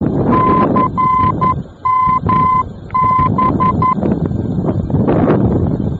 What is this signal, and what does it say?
Morse code, heard as a keyed tone, about 11 wpm, 1020 Hz, with noise. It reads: CMB
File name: france / mx / CMB-F-VOR